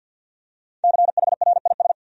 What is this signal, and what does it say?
Morse code, heard as a keyed tone, about 53 wpm, 700 Hz. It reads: CHRIS
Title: USA Common Names - 50wpm